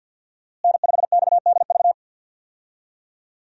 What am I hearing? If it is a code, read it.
Morse code, heard as a keyed tone, about 52 wpm, 700 Hz. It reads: N5XDV
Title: USA Call Signs - 5 Characters - Repeated 2x - 50wpm